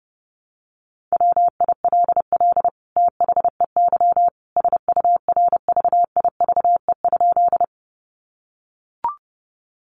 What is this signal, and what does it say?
Morse code, heard as a keyed tone, about 30 wpm, 700 Hz. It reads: WILL THEY SURVIVE?
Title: Sentences from Top 1300 Words 30wpm